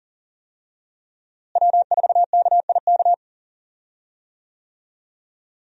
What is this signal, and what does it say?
Morse code, heard as a keyed tone, about 40 wpm, 700 Hz. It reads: W4KIK
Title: USA Call Signs - 5 Characters - Repeated 2x - 40wpm